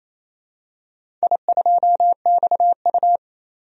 Morse code, heard as a keyed tone, about 28 wpm, 700 Hz. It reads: I2XU
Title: DXCC-100 Call Signs - 4 Characters - 28wpm